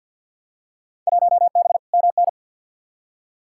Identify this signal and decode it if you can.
Morse code, heard as a keyed tone, about 52 wpm, 700 Hz. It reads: 1B MN
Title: Field Day Exchanges Encore - 50wpm